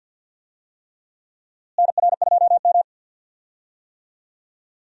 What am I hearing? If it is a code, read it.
Morse code, heard as a keyed tone, about 49 wpm, 700 Hz. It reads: NR2M
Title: USA Call Signs - 4 Characters - Repeated 3x - 50wpm